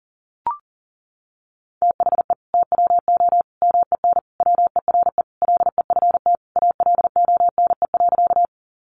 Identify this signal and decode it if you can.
Morse code, heard as a keyed tone, about 40 wpm, 700 Hz. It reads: THE TWO MEN WERE LEFT ALONE.
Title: Sentences from Top 1000 Words 40wpm